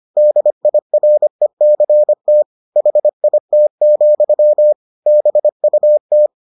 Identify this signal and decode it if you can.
Morse code, heard as a keyed tone, about 25 wpm, 600 Hz. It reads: DIRECT HIT, BUT